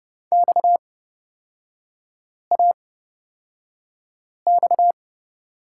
Morse code, heard as a keyed tone, about 29 wpm, spaced as for 5 wpm, 700 Hz. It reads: XAX
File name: Lesson 37e - Letter x - Three Characters 30wpm